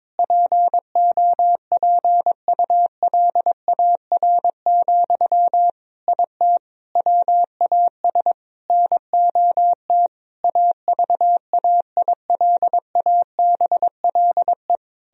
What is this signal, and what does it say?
Morse code, heard as a keyed tone, about 22 wpm, 700 Hz. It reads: POPULAR, IT WAS NOT AVAILABLE
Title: Sentences from Top 1000 Words - Review Part 3 - 22wpm